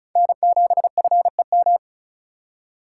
Morse code, heard as a keyed tone, about 36 wpm, 700 Hz. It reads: N7FEM
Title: USA Call Signs - 4x Character Spacing - 35wpm